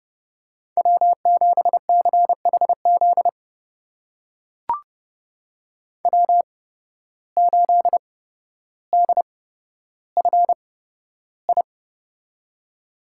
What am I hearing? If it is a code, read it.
Morse code, heard as a keyed tone, about 30 wpm, spaced as for 13 wpm, 700 Hz. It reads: W7CHZ W8DFI
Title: USA Call Signs - 5 Characters - 8x Character Spacing - 30wpm